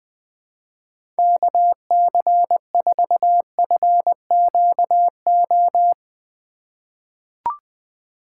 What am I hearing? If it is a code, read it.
Morse code, heard as a keyed tone, about 20 wpm, 700 Hz. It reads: KC4FQO
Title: USA Call Signs - 6 Characters - 3x Character Spacing - 20wpm